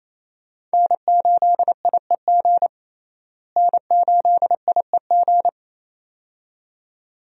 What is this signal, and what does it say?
Morse code, heard as a keyed tone, about 28 wpm, 700 Hz. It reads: N8IEG N8IEG
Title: USA Call Signs - 5 Characters - Repeated 2x - 28wpm